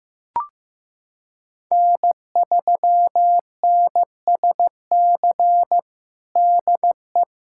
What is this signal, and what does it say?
Morse code, heard as a keyed tone, about 15 wpm, 700 Hz. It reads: N3NSC DE
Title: USA Callsign DE Callsign - 5 Characters - 15wpm